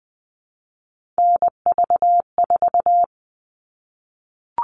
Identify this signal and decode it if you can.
Morse code, heard as a keyed tone, about 20 wpm, 700 Hz. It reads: NV4
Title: All Cut Numbers - 3-Digit 20wpm